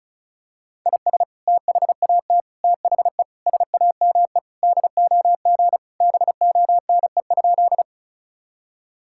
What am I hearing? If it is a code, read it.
Morse code, heard as a keyed tone, about 36 wpm, 700 Hz. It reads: IS THAT THE SAME DOG BONE?
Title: Sentences from Top 1600 Words 35wpm